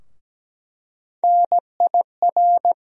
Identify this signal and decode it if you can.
Morse code, heard as a keyed tone, about 17 wpm, 700 Hz. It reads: NIR
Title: USA Call Sign Postfixes - 3 Letters 17wpm